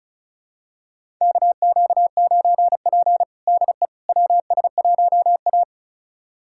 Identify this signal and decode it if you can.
Morse code, heard as a keyed tone, about 34 wpm, 700 Hz. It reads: KQ9P DE WS1A
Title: USA Callsign DE Callsign - 4 Characters - 35wpm